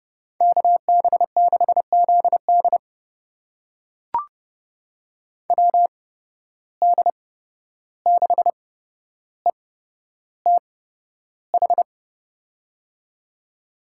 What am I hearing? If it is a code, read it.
Morse code, heard as a keyed tone, about 32 wpm, spaced as for 12 wpm, 700 Hz. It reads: KB6ZD WD6ETH
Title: USA Call Signs - 8x Character Spacing - 30wpm